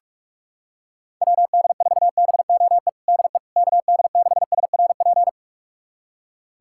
Morse code, heard as a keyed tone, about 42 wpm, 700 Hz. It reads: WD4BOE DE KD6SRP
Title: USA Callsign DE Callsign 45wpm